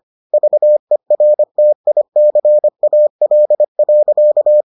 Morse code, heard as a keyed tone, about 25 wpm, 600 Hz. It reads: VERTICAL.